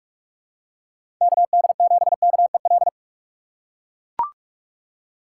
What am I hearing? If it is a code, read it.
Morse code, heard as a keyed tone, about 43 wpm, 700 Hz. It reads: KD7KEL
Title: USA Call Signs - 3x Character Spacing - 45wpm